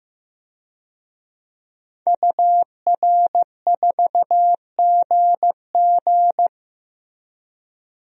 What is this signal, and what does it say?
Morse code, heard as a keyed tone, about 16 wpm, 700 Hz. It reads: UR4GG
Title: DXCC-100 Call Signs - 5 Characters - Repeated 3x - 15wpm